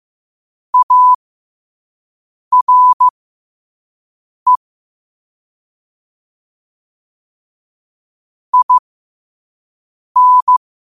Morse code, heard as a keyed tone, about 15 wpm, spaced as for 5 wpm, 1000 Hz. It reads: ARE IN
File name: qotdfiles / 5wpm